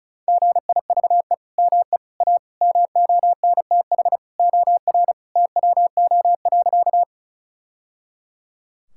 Morse code, heard as a keyed tone, about 34 wpm, 700 Hz. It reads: GIVE ME A MONTH OR TWO.
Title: Sentences from Top 1000 Words - Review Part 5 - 35wpm